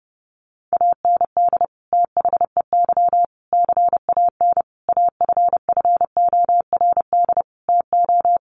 Morse code, heard as a keyed tone, about 30 wpm, 700 Hz. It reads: AND THEY CAN AFFORD TO
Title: Sentences from Top 1100 Words 30wpm